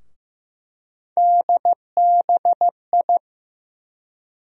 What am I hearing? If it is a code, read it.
Morse code, heard as a keyed tone, about 15 wpm, 700 Hz. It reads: DBI
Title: USA Call Sign Postfixes - 3 Letters 15wpm